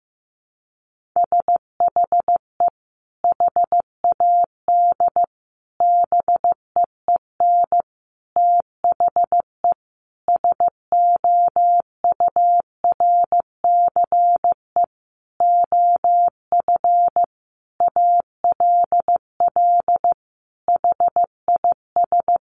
Morse code, heard as a keyed tone, about 15 wpm, 700 Hz. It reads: SHE HAD BEEN THE SOURCE OF ALL HIS
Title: Sentences from Top 1200 Words 15wpm